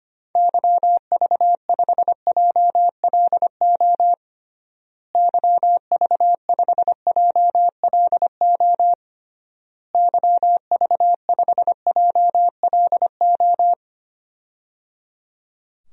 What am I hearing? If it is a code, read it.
Morse code, heard as a keyed tone, about 25 wpm, 700 Hz. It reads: YV5JLO YV5JLO YV5JLO
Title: DXCC-100 Call Signs - 6 Characters - Repeated 3x - 25wpm